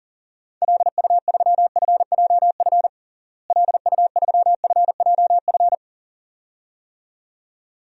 Morse code, heard as a keyed tone, about 40 wpm, 700 Hz. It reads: LU3FJF LU3FJF
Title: DXCC-100 Call Signs - 6 Characters - Repeated 2x - 40wpm